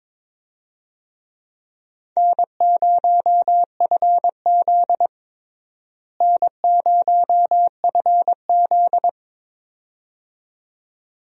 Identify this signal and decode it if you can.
Morse code, heard as a keyed tone, about 22 wpm, 700 Hz. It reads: N0FZ N0FZ
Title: USA Call Signs - 4 Characters - Repeated 2x - 22wpm